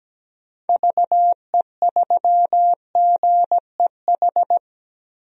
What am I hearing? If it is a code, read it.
Morse code, heard as a keyed tone, about 17 wpm, 700 Hz. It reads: VE3GEH
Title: DXCC-100 Call Signs - 6 Characters - Repeated 3x - 17wpm